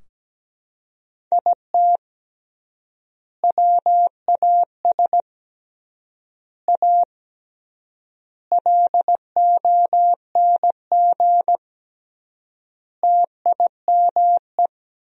Morse code, heard as a keyed tone, about 17 wpm, 700 Hz. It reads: IT WAS A LONG TIME
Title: 3x Word-Spacing - Sentences from Top 500 Words - 17wpm